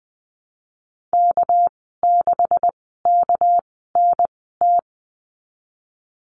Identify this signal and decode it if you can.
Morse code, heard as a keyed tone, about 19 wpm, spaced as for 14 wpm, 700 Hz. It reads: K6KNT
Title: 200 Lesson 32g - Letter k - Call Signs 20wpm